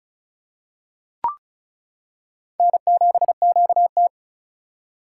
Morse code, heard as a keyed tone, about 36 wpm, 700 Hz. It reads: N7QT
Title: K1USN SST QSO Practice - 35wpm